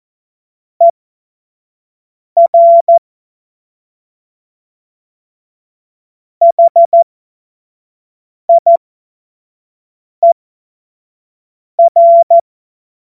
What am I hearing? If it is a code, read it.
Morse code, heard as a keyed tone, about 14 wpm, spaced as for 5 wpm, 680 Hz. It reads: ER HIER
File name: Morsekurs / Witze / Witz 03. Januar 2023 25 F70